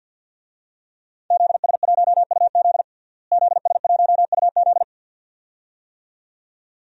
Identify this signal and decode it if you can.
Morse code, heard as a keyed tone, about 52 wpm, 700 Hz. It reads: ZS1UZ ZS1UZ
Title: DXCC-100 Call Signs - 5 Characters - Repeated 2x - 50wpm